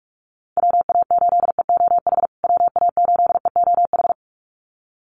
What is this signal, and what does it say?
Morse code, heard as a keyed tone, about 44 wpm, 700 Hz. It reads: WA8EOH WA8EOH
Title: USA Call Signs Repeated 2x 45wpm